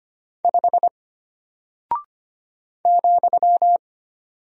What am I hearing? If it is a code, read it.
Morse code, heard as a keyed tone, about 25 wpm, 700 Hz. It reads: 5 ,
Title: Single Character (Letter:Number:Punctuation:Prosign) Alt 25wpm